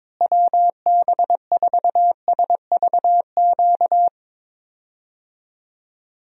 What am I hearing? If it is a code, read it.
Morse code, heard as a keyed tone, about 22 wpm, 700 Hz. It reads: WB4SVQ
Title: USA Call Signs - 6 Characters - Repeated 3x - 22wpm